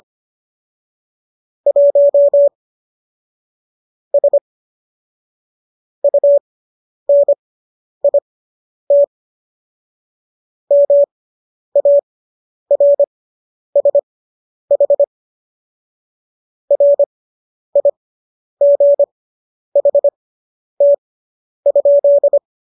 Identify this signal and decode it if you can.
Morse code, heard as a keyed tone, about 25 wpm, spaced as for 10 wpm, 570 Hz. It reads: 1 S UNIT MARSH RIGHT?